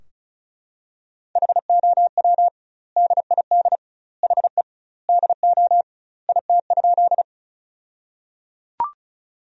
Morse code, heard as a keyed tone, about 36 wpm, 700 Hz. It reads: HOW DID HE DO IT?
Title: 2x Word-Spacing - Sentences from Top 100 Words - 35wpm